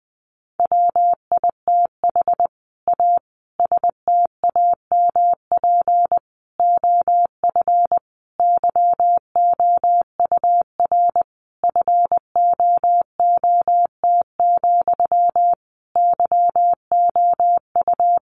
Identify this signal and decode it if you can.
Morse code, heard as a keyed tone, about 20 wpm, 700 Hz. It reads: WITH A STAMP OF YOUR FOOT, YOU
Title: A Sound Of Thunder 20wpm